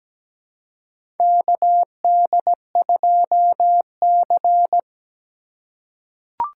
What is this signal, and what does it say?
Morse code, heard as a keyed tone, about 17 wpm, 700 Hz. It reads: KD2C
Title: USA Call Signs - 3x Character Spacing - 17wpm